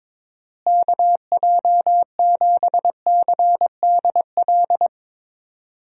Morse code, heard as a keyed tone, about 22 wpm, 700 Hz. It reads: KJ7CDL
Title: USA Call Signs - 6 Characters - Repeated 3x - 22wpm